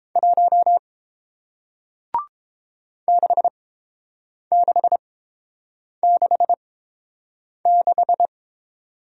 Morse code, heard as a keyed tone, about 30 wpm, spaced as for 7 wpm, 700 Hz. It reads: 1 6666